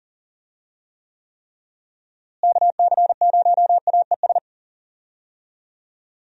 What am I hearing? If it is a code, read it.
Morse code, heard as a keyed tone, about 40 wpm, 700 Hz. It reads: KC0AES
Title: USA Call Signs - 6 Characters - Repeated 3x - 40wpm